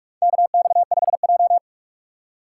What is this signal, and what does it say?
Morse code, heard as a keyed tone, about 46 wpm, 700 Hz. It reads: KX5J
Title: USA Call Signs - 3x Character Spacing - 45wpm